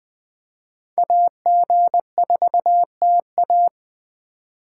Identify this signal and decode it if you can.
Morse code, heard as a keyed tone, about 20 wpm, 700 Hz. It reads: AG4TA
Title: Speed Racing - USA 2x2 Call Signs 17wpm